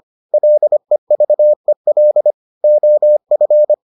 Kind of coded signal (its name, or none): Morse code